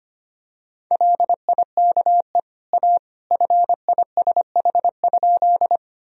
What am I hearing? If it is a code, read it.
Morse code, heard as a keyed tone, about 25 wpm, 700 Hz. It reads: LIKE A FISH?
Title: Sentences from Top 1000 Words - Review Part 1 - 25wpm